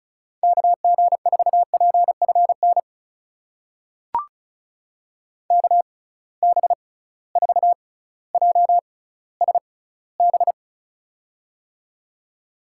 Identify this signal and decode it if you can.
Morse code, heard as a keyed tone, about 34 wpm, spaced as for 18 wpm, 700 Hz. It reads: KG4PFN KB4JSB
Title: USA Call Signs - 6 Characters - 6x Character Spacing - 35wpm